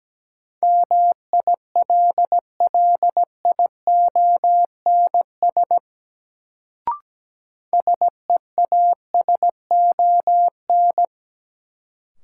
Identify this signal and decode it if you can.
Morse code, heard as a keyed tone, about 17 wpm, 700 Hz. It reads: MILLIONS SEASON
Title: Sets of 5 Words - New Word Drill - 17wpm